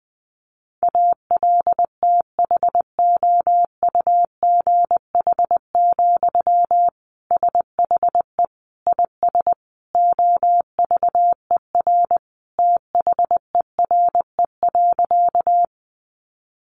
Morse code, heard as a keyed tone, about 20 wpm, 700 Hz. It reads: ALTHOUGH, SHE IS OVER THERE.
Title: Sentences from Top 600 Words 20wpm